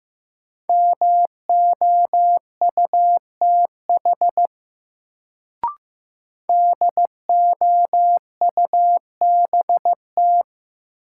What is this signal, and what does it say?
Morse code, heard as a keyed tone, about 15 wpm, 700 Hz. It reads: MOUTH DOUBT
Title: Sentences from Top 1500 Words - New Word Drill Alt - 15wpm